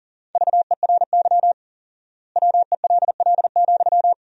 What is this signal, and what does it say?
Morse code, heard as a keyed tone, about 40 wpm, 700 Hz. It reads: VERY WELL,
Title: 4x Word-Spacing - Sentences from Top 200 Words - 40wpm